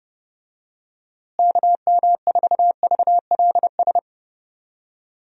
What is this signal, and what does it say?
Morse code, heard as a keyed tone, about 30 wpm, 700 Hz. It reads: KM4VLS
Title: USA Call Signs - 6 Characters - Repeated 3x - 30wpm